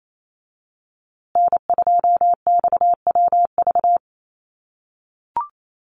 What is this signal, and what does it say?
Morse code, heard as a keyed tone, about 28 wpm, 700 Hz. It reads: N2XWV
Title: USA Call Signs Repeated 3x 28wpm